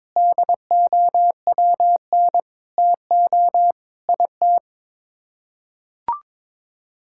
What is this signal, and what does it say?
Morse code, heard as a keyed tone, about 22 wpm, 700 Hz. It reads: DOWN TO IT
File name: practice / 22wpm / Sets of 6 Words - 22wpm